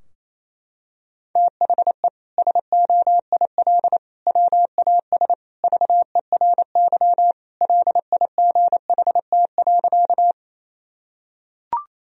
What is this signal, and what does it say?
Morse code, heard as a keyed tone, about 28 wpm, 700 Hz. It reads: THE SOIL WAS VERY LIGHT.
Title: Sentences from Top 1000 Words - Review Part 4 - 28wpm